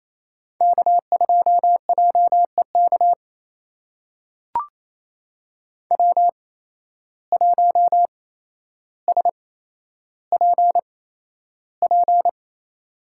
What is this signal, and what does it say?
Morse code, heard as a keyed tone, about 28 wpm, spaced as for 12 wpm, 700 Hz. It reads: K2JEK W1SPP
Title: USA Call Signs - 5 Characters - 8x Character Spacing - 28wpm